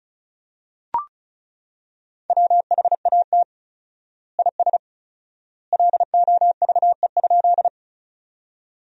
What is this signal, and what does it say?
Morse code, heard as a keyed tone, about 36 wpm, 700 Hz. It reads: WHAT IS LOVE?
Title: 4x Word-Spacing - Sentences from Top 400 Words - 35wpm